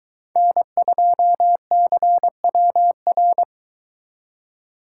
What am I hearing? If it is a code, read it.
Morse code, heard as a keyed tone, about 22 wpm, 700 Hz. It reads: N2CWR